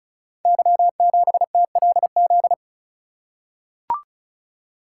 Morse code, heard as a keyed tone, about 36 wpm, 700 Hz. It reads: Y7TLZ
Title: Code Groups - 5 Letter:Numbers - 35wpm